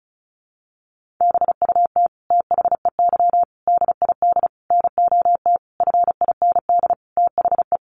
Morse code, heard as a keyed tone, about 34 wpm, 700 Hz. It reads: BUT THEY DID NOT FIND THE
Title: Sentences from Top 1400 Words 35wpm